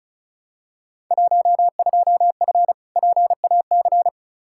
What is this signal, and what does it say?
Morse code, heard as a keyed tone, about 35 wpm, 700 Hz. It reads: 12F PAC
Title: Field Day Exchanges Encore - 35wpm